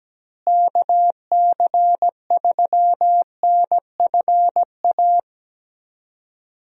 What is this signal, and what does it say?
Morse code, heard as a keyed tone, about 17 wpm, 700 Hz. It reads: KC3NFA
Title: USA Call Signs - 6 Characters - Repeated 2x - 17wpm